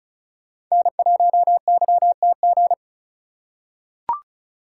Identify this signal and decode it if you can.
Morse code, heard as a keyed tone, about 34 wpm, 700 Hz. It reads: N1YTG
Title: USA Call Signs - 5 Characters - 3x Character Spacing - 35wpm